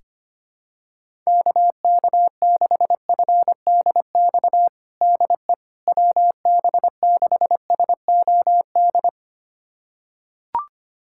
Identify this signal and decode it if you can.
Morse code, heard as a keyed tone, about 25 wpm, 700 Hz. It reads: KK6FDX DE WB6SOD
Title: USA Callsign DE Callsign - 6 Characters - 25wpm